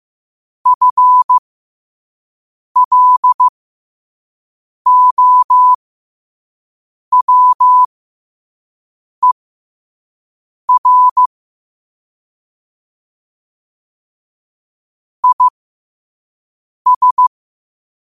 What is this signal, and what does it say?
Morse code, heard as a keyed tone, about 15 wpm, spaced as for 5 wpm, 1000 Hz. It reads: FLOWER IS
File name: qotdfiles / 5wpm